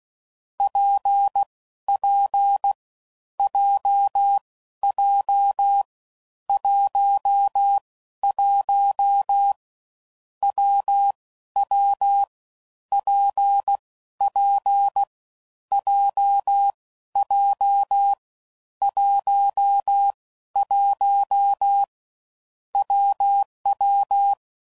Morse code, heard as a keyed tone, about 16 wpm, 800 Hz. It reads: P P J J 1 1 W W P P J J 1 1 WW
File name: Combinations / Difficult Combos / WPJ WPJ1